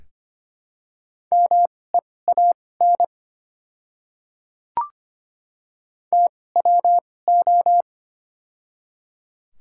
Morse code, heard as a keyed tone, about 25 wpm, spaced as for 18 wpm, 690 Hz. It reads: MEAN TWO
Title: 2x Character Spacing - Top 200 Words - 25wpm